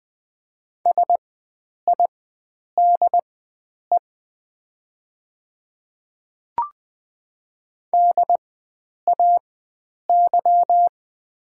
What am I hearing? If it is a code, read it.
Morse code, heard as a keyed tone, about 20 wpm, spaced as for 9 wpm, 700 Hz. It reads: SIDE DAY
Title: 4x Character Spacing - Top 100 Words - 20wpm